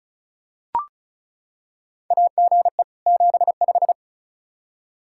Morse code, heard as a keyed tone, about 36 wpm, 700 Hz. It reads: AGE 75
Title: K1USN SST QSO Practice 35wpm